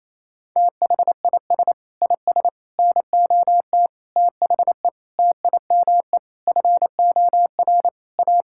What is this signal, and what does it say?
Morse code, heard as a keyed tone, about 28 wpm, 700 Hz. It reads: THIS IS NOT THE TIME FOR A
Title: Sentences from Top 1000 Words - Review Part 5 - 28wpm